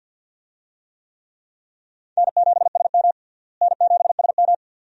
Morse code, heard as a keyed tone, about 53 wpm, 700 Hz. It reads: N7SM N7SM
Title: USA Call Signs - 4 Characters - Repeated 2x - 50wpm